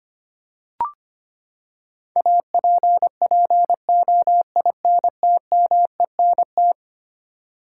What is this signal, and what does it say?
Morse code, heard as a keyed tone, about 25 wpm, 700 Hz. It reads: APPOINTMENT
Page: morse-code-ninja-practice-25wpm